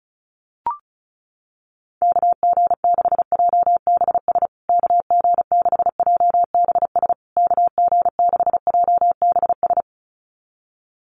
Morse code, heard as a keyed tone, about 34 wpm, 700 Hz. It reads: KG6JBS KG6JBS KG6JBS
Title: USA Call Signs Repeated 3x 35wpm